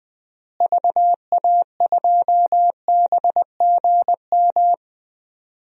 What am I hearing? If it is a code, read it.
Morse code, heard as a keyed tone, about 20 wpm, 700 Hz. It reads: VA2BGM
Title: DXCC-100 Call Signs - 6 Characters - Repeated 2x - 20wpm